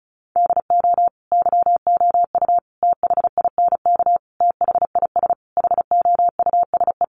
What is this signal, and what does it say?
Morse code, heard as a keyed tone, about 34 wpm, 700 Hz. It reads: DO YOU THINK THIS HOUSE
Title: Sentences from Top 1000 Words 35wpm